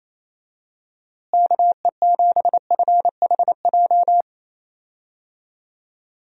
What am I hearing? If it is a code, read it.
Morse code, heard as a keyed tone, about 28 wpm, 700 Hz. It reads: KE7FHJ
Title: USA Call Signs - 6 Characters - Repeated 2x - 28wpm